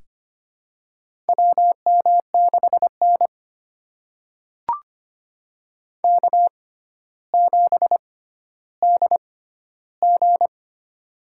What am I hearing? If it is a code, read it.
Morse code, heard as a keyed tone, about 25 wpm, spaced as for 13 wpm, 700 Hz. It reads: WM6N K7DG
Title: USA Call Signs - 4 Characters - 6x Character Spacing - 25wpm